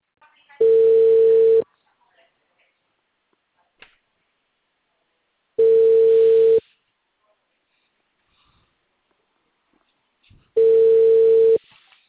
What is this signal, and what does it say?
Ringback tone (China, 450 Hz).
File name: out- -2012- - - .7611